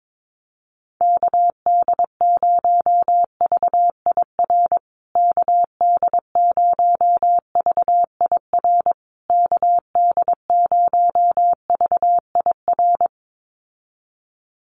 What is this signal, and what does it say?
Morse code, heard as a keyed tone, about 22 wpm, 700 Hz. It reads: KD0VIR KD0VIR KD0VIR
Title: USA Call Signs Repeated 3x 22wpm